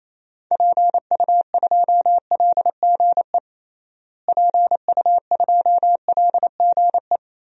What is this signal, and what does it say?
Morse code, heard as a keyed tone, about 28 wpm, 700 Hz. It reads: PU2LGE PU2LGE
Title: DXCC-100 Call Signs - 6 Characters - Repeated 2x - 28wpm